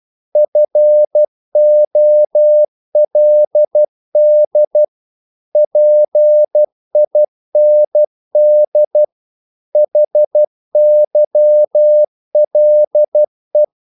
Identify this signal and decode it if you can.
Morse code, heard as a keyed tone, about 12 wpm, 600 Hz. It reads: FOLD PIND HYLE
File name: Korte ord DK 12wpm